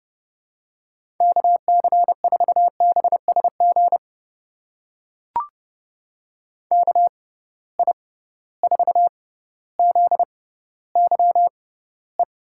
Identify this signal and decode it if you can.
Morse code, heard as a keyed tone, about 30 wpm, spaced as for 15 wpm, 700 Hz. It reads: KC4BSG KI4ZYE
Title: USA Call Signs - 6 Characters - 6x Character Spacing - 30wpm